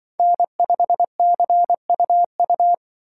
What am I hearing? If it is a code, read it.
Morse code, heard as a keyed tone, about 24 wpm, 700 Hz. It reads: N5CUU
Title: Speed Racing - USA Call Signs 20wpm